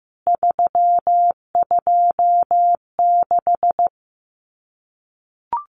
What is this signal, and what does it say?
Morse code, heard as a keyed tone, about 15 wpm, 700 Hz. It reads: 326
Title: CWT Exchange 15wpm